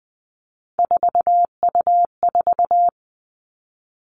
Morse code, heard as a keyed tone, about 20 wpm, 700 Hz. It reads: 4U4
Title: Three Challenging Letter-Number Combinations 20wpm